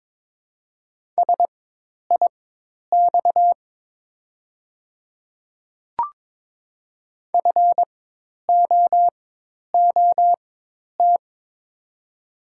Morse code, heard as a keyed tone, about 23 wpm, spaced as for 10 wpm, 700 Hz. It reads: SIX FOOT
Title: 4x Character Spacing - Top 500 Words - 22wpm